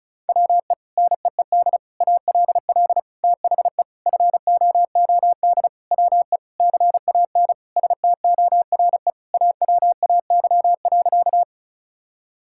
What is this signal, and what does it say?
Morse code, heard as a keyed tone, about 36 wpm, 700 Hz. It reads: WE NEED ALL THE FOOD WE CAN STORE AWAY.
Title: Sentences from Top 1000 Words - Review Part 5 - 35wpm